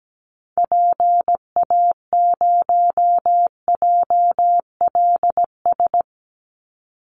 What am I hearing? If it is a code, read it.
Morse code, heard as a keyed tone, about 17 wpm, 700 Hz. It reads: PA0JLS
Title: Call Signs Repeated 3x - 17wpm